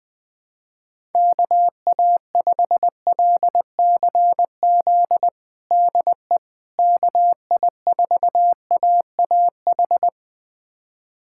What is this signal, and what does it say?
Morse code, heard as a keyed tone, about 20 wpm, 700 Hz. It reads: KA5LCZ DE KI4AAH
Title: USA Callsign DE Callsign - 6 Characters - 20wpm